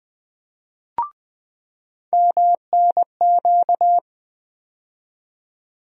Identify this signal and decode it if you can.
Morse code, heard as a keyed tone, about 20 wpm, 700 Hz. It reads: MNQ
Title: USA Call Sign Postfixes - 3 Letters 20wpm